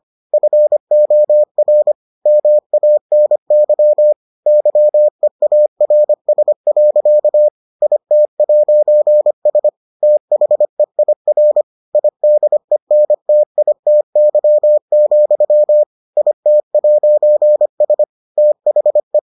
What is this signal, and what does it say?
Morse code, heard as a keyed tone, about 25 wpm, 600 Hz. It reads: FOR MANY YEARS. IT'S THEIR IDENTITY, IT'S THE